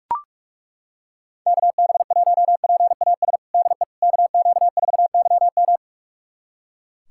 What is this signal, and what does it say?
Morse code, heard as a keyed tone, about 42 wpm, 700 Hz. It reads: KB1PAS DE KQ4YM
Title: USA Callsign DE Callsign 45wpm